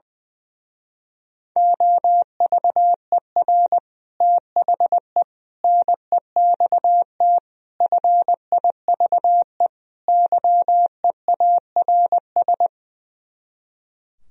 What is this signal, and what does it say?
Morse code, heard as a keyed tone, about 21 wpm, 700 Hz. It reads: OVER THE NEXT FIVE YEARS
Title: Speed-Racing - Sets of 5 Words 20wpm